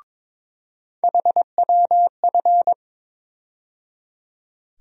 Morse code, heard as a keyed tone, about 22 wpm, 700 Hz. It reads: HWF
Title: USA Call Sign Postfixes - 3 Letters 22wpm